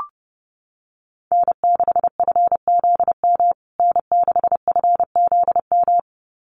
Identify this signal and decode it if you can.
Morse code, heard as a keyed tone, about 30 wpm, 700 Hz. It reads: N6FZM N6FZM
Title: USA Call Signs Repeated 2x 30wpm